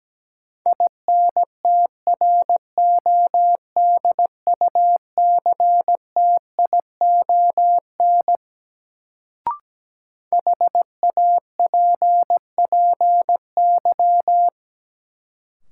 Morse code, heard as a keyed tone, about 17 wpm, 700 Hz. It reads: INTRODUCTION HAPPY
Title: Sentences from Top 1000 Words - New Word Drill - 17wpm